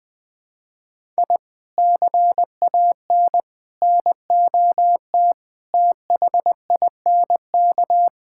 Morse code, heard as a keyed tone, about 20 wpm, 700 Hz. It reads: I CAN NOT THINK
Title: Build A Phrase - 5-Word Phrases - 20wpm